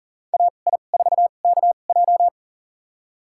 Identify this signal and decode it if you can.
Morse code, heard as a keyed tone, about 40 wpm, spaced as for 29 wpm, 700 Hz. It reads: AI4KJ
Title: USA Call Signs - 2x Character Spacing - 40wpm